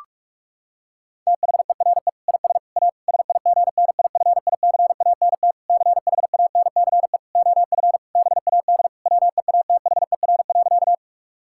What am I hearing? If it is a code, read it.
Morse code, heard as a keyed tone, about 46 wpm, 700 Hz. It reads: THERE IS A SIGNIFICANT CHANCE OF BAD WEATHER.
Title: Sentences from Top 1000 Words - Review Part 6 - 45wpm